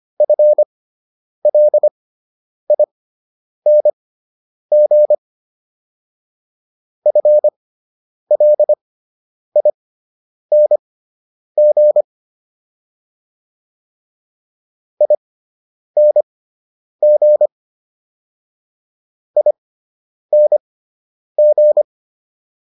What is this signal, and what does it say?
Morse code, heard as a keyed tone, about 25 wpm, spaced as for 9 wpm, 600 Hz. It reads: FLING FLING ING ING
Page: /fundamental-practice-files/